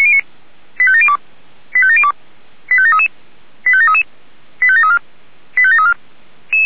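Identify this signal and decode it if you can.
Five-tone selective call (ZVEI), heard as quick runs of five tones, tones of about 70 ms. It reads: 85702 85702 85733 85733 85734 85734
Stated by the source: ZVEI five-tone selective call 85737 received by SDR.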